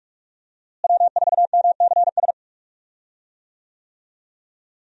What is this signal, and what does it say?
Morse code, heard as a keyed tone, about 47 wpm, 700 Hz. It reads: W4MCS
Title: USA Call Signs - 5 Characters - Repeated 3x - 45wpm